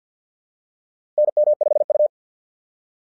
Morse code, heard as a keyed tone, about 50 wpm, 600 Hz. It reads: NM5U
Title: SOTA QSO Practice 50wpm